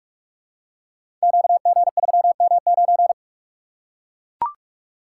Morse code, heard as a keyed tone, about 44 wpm, 700 Hz. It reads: QG3M9
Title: Code Groups - 5 Letter:Numbers - 45wpm